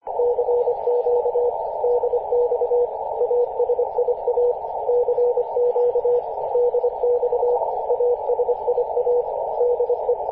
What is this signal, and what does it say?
Morse code, heard as a keyed tone, about 24 wpm, 490 Hz, with radio noise. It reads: CQ DX ASIA CQ DX ASIA DE
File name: posts / 2016-july-01-0720z-ut7qf-50-096mhz-cw